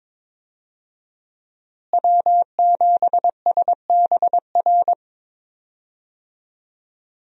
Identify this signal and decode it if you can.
Morse code, heard as a keyed tone, about 22 wpm, 700 Hz. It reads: W7SBR
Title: USA Call Signs - 5 Characters - Repeated 3x - 22wpm